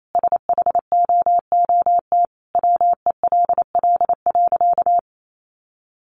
Morse code, heard as a keyed tone, about 28 wpm, 700 Hz. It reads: SHOOT WELL.
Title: Sentences from Top 1200 Words 28wpm